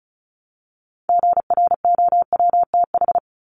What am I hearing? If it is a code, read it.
Morse code, heard as a keyed tone, about 34 wpm, 700 Hz. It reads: GROWTH
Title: Top 751-1000 Words 35wpm